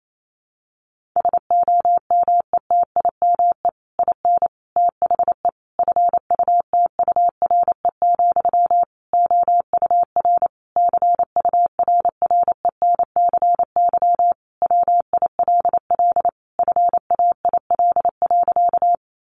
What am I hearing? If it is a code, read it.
Morse code, heard as a keyed tone, about 28 wpm, 700 Hz. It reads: SOMETIME IN THE FUTURE, OUR CURRENCY WILL FAIL.
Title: Sentences from Top 1500 Words 28wpm